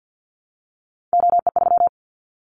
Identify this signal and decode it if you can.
Morse code, heard as a keyed tone, about 50 wpm, 700 Hz. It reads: OE3
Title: Callsign Prefixes for Top 50 DXCC - 50wpm